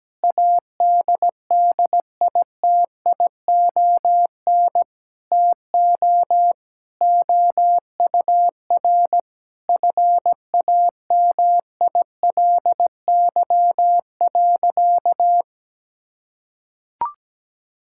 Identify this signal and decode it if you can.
Morse code, heard as a keyed tone, about 17 wpm, 700 Hz. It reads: ADDITION TO OUR FAMILY.
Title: Sentences from Top 1000 Words - Review Part 6 - 17wpm